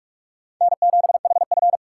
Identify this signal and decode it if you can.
Morse code, heard as a keyed tone, about 56 wpm, 700 Hz. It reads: N7HF